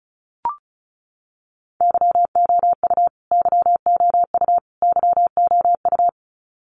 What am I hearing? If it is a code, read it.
Morse code, heard as a keyed tone, about 34 wpm, 700 Hz. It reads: YOU YOU YOU
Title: Top 300 Words Repeated 35wpm